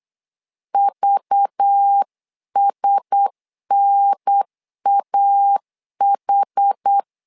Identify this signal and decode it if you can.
Morse code, heard as a keyed tone, about 8 wpm, 800 Hz. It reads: VSNAH
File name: LEZIONI / 5 LBVC / 5 LBV 8wpm 2837